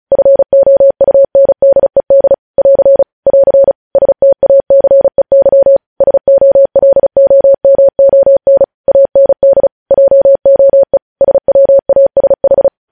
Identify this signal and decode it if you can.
Morse code, heard as a keyed tone, about 36 wpm, 550 Hz. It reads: FOUNDED <AR> <AR> STACEY SOLOMON AND JOE SWASH
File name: podcast / News headlines-1x-35x35